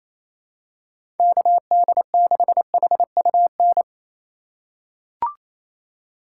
Morse code, heard as a keyed tone, about 28 wpm, 700 Hz. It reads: KD6HUN
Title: USA Call Signs - 6 Characters - 3x Character Spacing - 28wpm